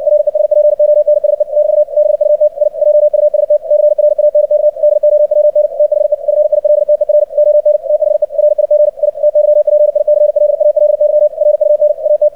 Morse code, heard as a keyed tone, about 60 wpm, 600 Hz, with radio noise. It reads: RESPONSE <BT> BUT I JUST CRANK DOWN THE FILTER ON THE RIG I TYPICLALY RUN AT
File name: archived / CRSnet